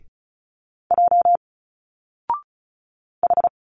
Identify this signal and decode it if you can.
Morse code, heard as a keyed tone, about 34 wpm, 700 Hz. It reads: J H
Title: Single Letters 35wpm